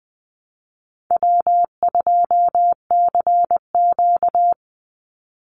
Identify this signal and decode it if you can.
Morse code, heard as a keyed tone, about 20 wpm, 700 Hz. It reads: W2CQ
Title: Easy Call Signs Repeated 2x - 20wpm